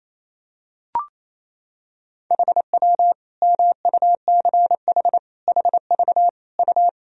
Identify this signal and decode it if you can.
Morse code, heard as a keyed tone, about 28 wpm, 700 Hz. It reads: HW MUCH HV U
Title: Sentences from Top 300 Words and Abbreviations 28wpm